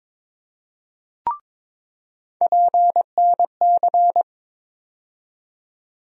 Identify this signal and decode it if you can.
Morse code, heard as a keyed tone, about 21 wpm, 700 Hz. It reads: PNC
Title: USA Call Sign Postfixes - 3 Letters 22wpm